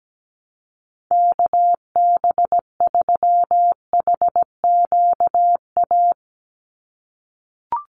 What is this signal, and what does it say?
Morse code, heard as a keyed tone, about 17 wpm, 700 Hz. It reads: KB3HQA
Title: USA Call Signs 17wpm